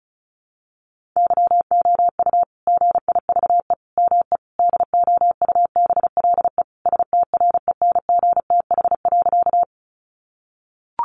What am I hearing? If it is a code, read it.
Morse code, heard as a keyed tone, about 34 wpm, 700 Hz. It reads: YOU GIVE ME DOUBLE STRENGTH.
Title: Sentences from Top 1400 Words 35wpm